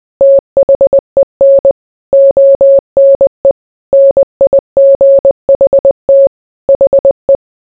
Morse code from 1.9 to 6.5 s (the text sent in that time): ONE NIGHT